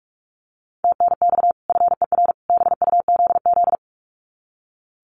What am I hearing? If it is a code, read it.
Morse code, heard as a keyed tone, about 44 wpm, 700 Hz. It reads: TNX FER BUZZ
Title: Common QSO Phrases 45wpm